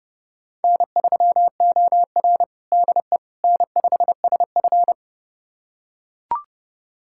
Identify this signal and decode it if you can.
Morse code, heard as a keyed tone, about 31 wpm, 700 Hz. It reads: N3OR DE N5SF
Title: USA Callsign DE Callsign - 4 Characters - 30wpm